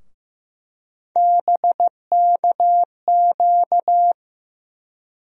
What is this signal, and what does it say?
Morse code, heard as a keyed tone, about 16 wpm, 700 Hz. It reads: BKQ
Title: USA Call Sign Postfixes - 3 Letters 15wpm